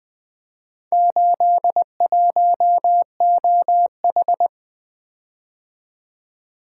Morse code, heard as a keyed tone, about 20 wpm, 700 Hz. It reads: 81OH
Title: Four Letter-Number Combinations - 20wpm